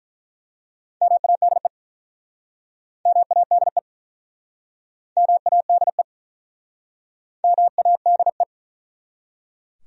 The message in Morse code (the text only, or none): MADE MADE MADE MADE